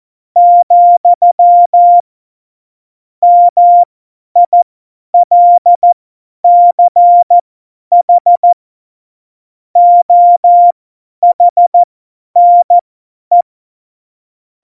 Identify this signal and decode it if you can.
Morse code, heard as a keyed tone, about 14 wpm, spaced as for 10 wpm, 700 Hz. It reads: , MILCH OHNE
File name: Morsekurs / Witze / Witz 29. Mai 2018, 50 F70